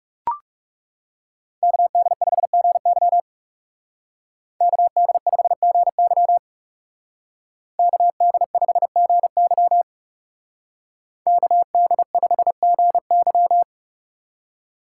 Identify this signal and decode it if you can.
Morse code, heard as a keyed tone, about 35 wpm, 700 Hz. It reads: KD5GY KD5GY KD5GY KD5GY
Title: Speed Racing - USA 2x2 Call Signs 30wpm